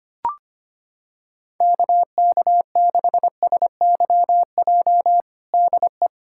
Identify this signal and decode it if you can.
Morse code, heard as a keyed tone, about 25 wpm, 700 Hz. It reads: KK6SYJ DE
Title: USA Callsign DE Callsign - 6 Characters - 25wpm